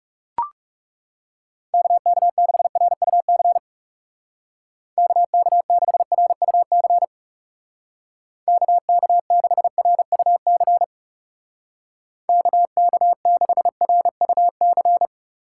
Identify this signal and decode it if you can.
Morse code, heard as a keyed tone, about 39 wpm, 700 Hz. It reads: KK6RUC KK6RUC KK6RUC KK6RUC